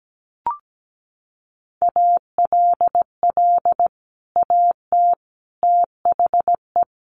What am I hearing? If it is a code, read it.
Morse code, heard as a keyed tone, about 17 wpm, 700 Hz. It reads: ALL AT THE
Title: Sets of 5 Words Encore2 17wpm